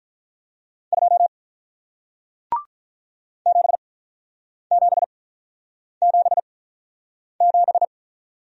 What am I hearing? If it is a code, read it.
Morse code, heard as a keyed tone, about 48 wpm, spaced as for 9 wpm, 700 Hz. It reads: 2 7777